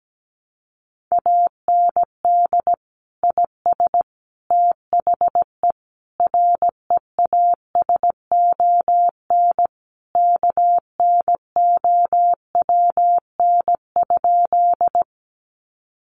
Morse code, heard as a keyed tone, about 17 wpm, 700 Hz. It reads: AND IS THE REASON KNOWN?
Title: Sentences from Top 700 Words 17wpm